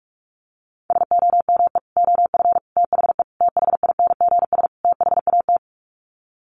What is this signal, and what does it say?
Morse code, heard as a keyed tone, about 44 wpm, 700 Hz. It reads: SOME OF THE THINGS THAT
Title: Sets of 5 Words 45wpm